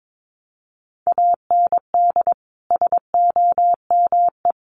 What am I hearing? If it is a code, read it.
Morse code, heard as a keyed tone, about 22 wpm, 700 Hz. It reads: AND SOME
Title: Sentences from Top 1300 Words 22wpm